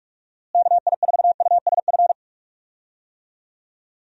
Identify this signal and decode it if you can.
Morse code, heard as a keyed tone, about 43 wpm, 700 Hz. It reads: KI4USF
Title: USA Call Signs - 6 Characters - Repeated 2x - 45wpm